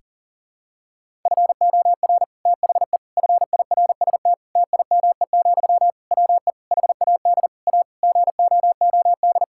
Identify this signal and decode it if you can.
Morse code, heard as a keyed tone, about 37 wpm, 700 Hz. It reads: FOR THE FIRST TIME, WE HAD A GOOD
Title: Sentences from Top 1000 Words - Review Part 6 - 40wpm